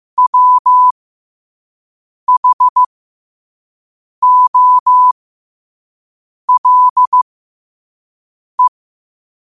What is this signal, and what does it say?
Morse code, heard as a keyed tone, about 15 wpm, spaced as for 5 wpm, 1000 Hz. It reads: WHOLE
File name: qotdfiles / 5wpm